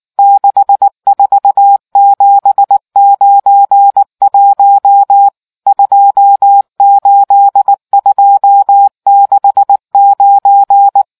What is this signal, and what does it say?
Morse code, heard as a keyed tone, about 19 wpm, 800 Hz. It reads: 64791 28269